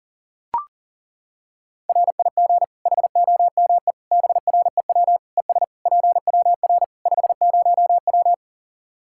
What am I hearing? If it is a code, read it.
Morse code, heard as a keyed tone, about 40 wpm, 700 Hz. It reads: RIG HOME BREW ES PWR 50W
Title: K1USN SST QSO Practice 40wpm